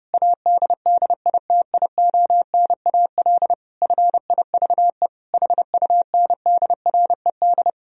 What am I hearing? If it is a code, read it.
Morse code, heard as a keyed tone, about 32 wpm, 700 Hz. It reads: ADDITIONAL FIVE HUNDRED
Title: Sentences from Top 1000 Words - Review Part 4 - 30wpm